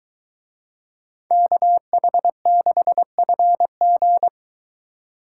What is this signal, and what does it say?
Morse code, heard as a keyed tone, about 23 wpm, 700 Hz. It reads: KH6FG